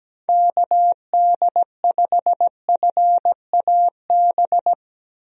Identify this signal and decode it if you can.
Morse code, heard as a keyed tone, about 17 wpm, 700 Hz. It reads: KD5FAB
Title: USA Call Signs - 6 Characters - Repeated 2x - 17wpm